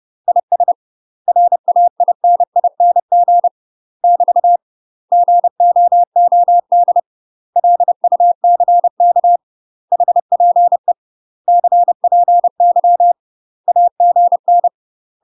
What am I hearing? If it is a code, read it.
Morse code, heard as a keyed tone, about 30 wpm, 700 Hz. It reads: IS RAINING <BT> GOOD LUCK HPE CPY AGN
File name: UK MASTER QSOs TRAINING / 30WPM / UK MASTER QSOs 0005